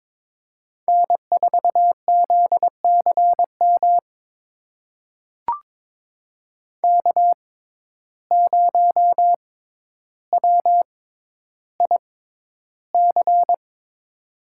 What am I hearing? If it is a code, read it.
Morse code, heard as a keyed tone, about 22 wpm, spaced as for 11 wpm, 700 Hz. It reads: N4ZCM K0WIC
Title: USA Call Signs - 5 Characters - 6x Character Spacing - 22wpm